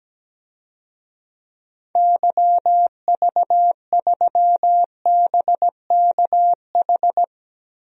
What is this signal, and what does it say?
Morse code, heard as a keyed tone, about 17 wpm, 700 Hz. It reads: YV3BKH
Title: DXCC-100 Call Signs - 6 Characters - 17wpm